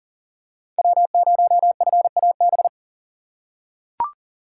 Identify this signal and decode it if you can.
Morse code, heard as a keyed tone, about 40 wpm, 700 Hz. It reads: W0FAB
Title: USA Call Signs - 5 Characters - 2x Character Spacing - 40wpm